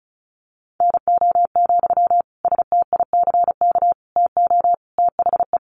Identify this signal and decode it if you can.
Morse code, heard as a keyed tone, about 34 wpm, 700 Hz. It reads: NO, STICK TO THE
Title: Sentences from Top 1400 Words 35wpm